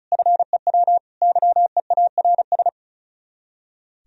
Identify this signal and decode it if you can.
Morse code, heard as a keyed tone, about 42 wpm, 700 Hz. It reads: FEW YEARS
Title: Speed-Racing - Sets of 3 Words 35wpm